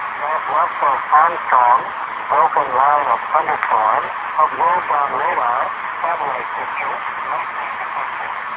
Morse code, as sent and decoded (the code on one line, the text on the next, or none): none